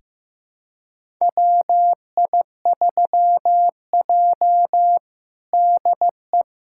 Morse code, heard as a keyed tone, about 15 wpm, 700 Hz. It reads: WI3J DE
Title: USA Callsign DE Callsign - 4 Characters - 15wpm